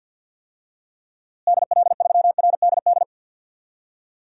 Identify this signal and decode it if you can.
Morse code, heard as a keyed tone, about 52 wpm, 700 Hz. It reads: DL3RDD
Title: DXCC-100 Call Signs - 6 Characters - Repeated 3x - 50wpm